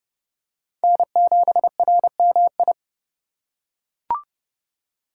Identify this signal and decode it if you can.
Morse code, heard as a keyed tone, about 29 wpm, 700 Hz. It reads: N7RMI